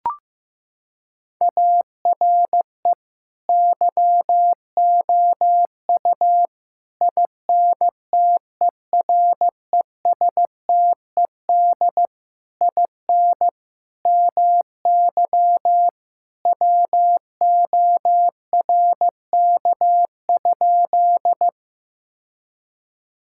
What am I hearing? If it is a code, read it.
Morse code, heard as a keyed tone, about 16 wpm, 700 Hz. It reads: ARE YOU INTERESTED IN MY WORK?
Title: Sentences from Top 1000 Words - Review Part 6 - 15wpm